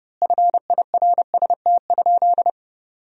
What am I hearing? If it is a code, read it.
Morse code, heard as a keyed tone, about 30 wpm, 700 Hz. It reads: FIRST?
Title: 2x Word-Spacing - Sentences from Top 100 Words - 30wpm